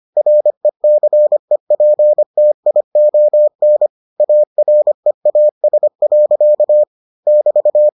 Morse code, heard as a keyed tone, about 25 wpm, 600 Hz. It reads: RECEPTION AREAS. <BT>